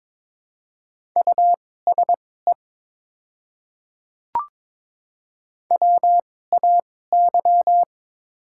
Morse code, heard as a keyed tone, about 21 wpm, spaced as for 16 wpm, 700 Hz. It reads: USE WAY
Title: 2x Character Spacing - Top 100 Words - 22wpm